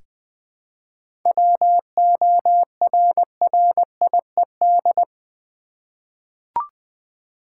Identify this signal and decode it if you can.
Morse code, heard as a keyed tone, about 20 wpm, 700 Hz. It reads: WORRIED
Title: Seven Letter Words - 20wpm